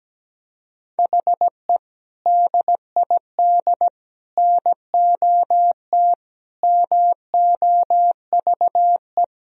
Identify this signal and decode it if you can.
Morse code, heard as a keyed tone, about 17 wpm, 700 Hz. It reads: HE DID NOT MOVE
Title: Sentences from Top 1000 Words - Review Part 1 - 17wpm